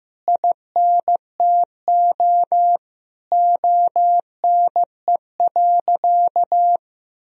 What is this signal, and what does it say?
Morse code, heard as a keyed tone, about 16 wpm, 700 Hz. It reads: INTO ONE.
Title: Sentences from Top 1000 Words - Review Part 1 - 15wpm